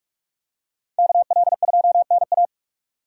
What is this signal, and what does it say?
Morse code, heard as a keyed tone, about 42 wpm, 700 Hz. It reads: KL2NA
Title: USA Call Signs - 5 Characters - 4x Character Spacing - 45wpm